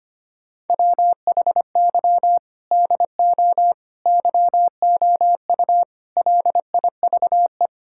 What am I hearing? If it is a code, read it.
Morse code, heard as a keyed tone, about 25 wpm, 700 Hz. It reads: WHY DO YOU LIVE
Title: Sentences from Top 1000 Words - Review Part 6 - 25wpm